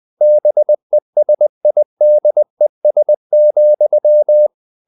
Morse code, heard as a keyed tone, about 20 wpm, 600 Hz. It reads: BESIDES,
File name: the old man and the sea part 2 20wpm